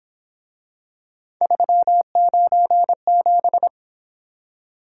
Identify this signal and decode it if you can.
Morse code, heard as a keyed tone, about 26 wpm, 700 Hz. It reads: 397
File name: Speed-Racing - Three Letter-Number Combinations - 22wpm